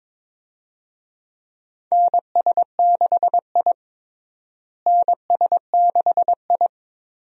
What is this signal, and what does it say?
Morse code, heard as a keyed tone, about 22 wpm, 700 Hz. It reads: NS6I NS6I
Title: USA Call Signs - 4 Characters - Repeated 2x - 22wpm